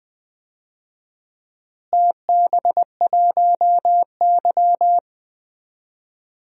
Morse code, heard as a keyed tone, about 20 wpm, 700 Hz. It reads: TB1Y
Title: DXCC-100 Call Signs - 4 Characters - Repeated 3x - 20wpm